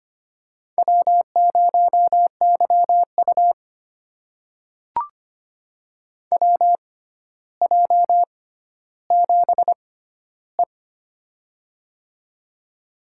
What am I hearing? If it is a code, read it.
Morse code, heard as a keyed tone, about 25 wpm, spaced as for 13 wpm, 700 Hz. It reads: W0YU WJ7E
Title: USA Call Signs - 4 Characters - 6x Character Spacing - 25wpm